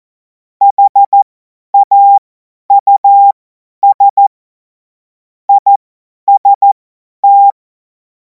Morse code, heard as a keyed tone, about 14 wpm, spaced as for 10 wpm, 800 Hz. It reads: HAUS IST
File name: Morsekurs / Witze / Witz 05. Mai 2026 50 F70